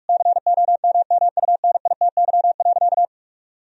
Morse code, heard as a keyed tone, about 44 wpm, 700 Hz. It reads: COMMUNITY.
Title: Sentences from Top 1000 Words - Review Part 4 - 45wpm